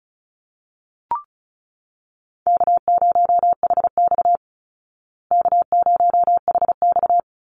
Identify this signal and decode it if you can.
Morse code, heard as a keyed tone, about 34 wpm, 700 Hz. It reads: K0HX K0HX
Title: Easy Call Signs Repeated 3x - 35wpm